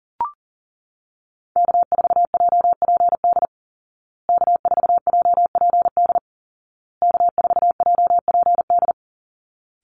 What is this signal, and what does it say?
Morse code, heard as a keyed tone, about 40 wpm, 700 Hz. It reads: K4JPD K4JPD K4JPD
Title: Call Signs Repeated 2x - 40wpm